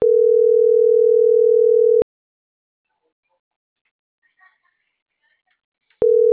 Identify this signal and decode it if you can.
Ringback tone (North America), 440 + 480 Hz.